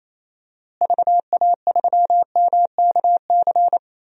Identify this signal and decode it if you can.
Morse code, heard as a keyed tone, about 27 wpm, 700 Hz. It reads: VA3MKC
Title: DXCC-100 Call Signs - 6 Characters - Repeated 3x - 28wpm